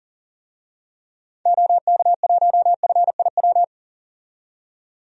Morse code, heard as a keyed tone, about 40 wpm, 700 Hz. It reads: OK1FIW
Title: DXCC-100 Call Signs - 6 Characters - 40wpm